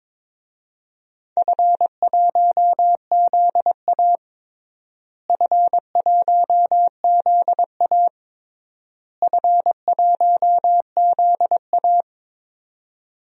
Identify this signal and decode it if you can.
Morse code, heard as a keyed tone, about 22 wpm, 700 Hz. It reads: F1ZA F1ZA F1ZA
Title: DXCC-100 Call Signs - 4 Characters - Repeated 3x - 22wpm